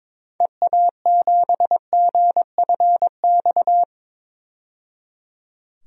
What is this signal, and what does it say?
Morse code, heard as a keyed tone, about 22 wpm, 700 Hz. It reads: EA7GFX
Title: DXCC-100 Call Signs - 6 Characters - 22wpm